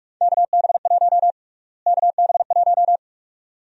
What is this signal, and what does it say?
Morse code, heard as a keyed tone, about 46 wpm, 700 Hz. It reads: KB1 KB1
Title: USA Call Sign Prefixes - Repeated 2x - 45wpm